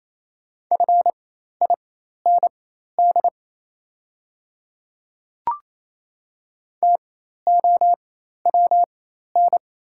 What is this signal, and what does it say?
Morse code, heard as a keyed tone, about 28 wpm, spaced as for 13 wpm, 690 Hz. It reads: FIND TOWN
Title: 4x Character Spacing - Top 500 Words - 28wpm